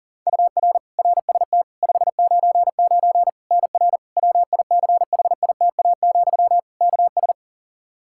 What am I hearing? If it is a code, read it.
Morse code, heard as a keyed tone, about 40 wpm, 700 Hz. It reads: UR RST 599 NR WICHITA, KS
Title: K1USN SST QSO Practice 40wpm